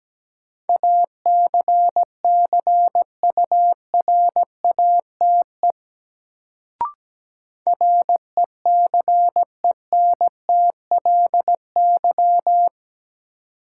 Morse code, heard as a keyed tone, about 17 wpm, 700 Hz. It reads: ACCURATE RECENTLY
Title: Sentences from Top 1200 Words - New Word Drill - 17wpm